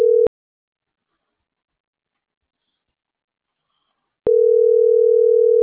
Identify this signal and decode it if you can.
Ringback tone (North America), 440 + 480 Hz.